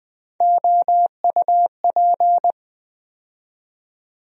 Morse code, heard as a keyed tone, about 20 wpm, 700 Hz. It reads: OUP
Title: USA Call Sign Postfixes - 3 Letters 20wpm